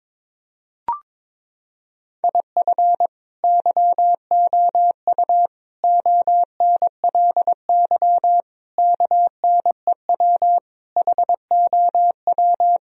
Morse code, heard as a keyed tone, about 22 wpm, 700 Hz. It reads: IF YOU ONLY KNEW HOW
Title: Sentences from Top 1000 Words - Review Part 4 - 22wpm